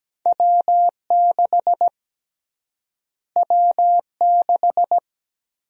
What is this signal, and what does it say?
Morse code, heard as a keyed tone, about 17 wpm, 700 Hz. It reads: W6 W6
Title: USA Call Sign Prefixes - Repeated 2x - 17wpm